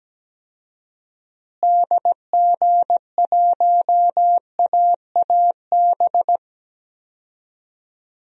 Morse code, heard as a keyed tone, about 17 wpm, 700 Hz. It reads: DG1AAB
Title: DXCC-100 Call Signs - 6 Characters - Repeated 2x - 17wpm